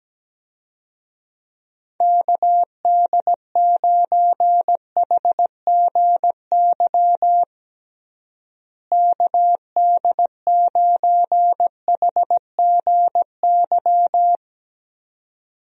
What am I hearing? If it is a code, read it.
Morse code, heard as a keyed tone, about 17 wpm, 700 Hz. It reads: KD9HGY KD9HGY
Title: USA Call Signs - 6 Characters - Repeated 2x - 17wpm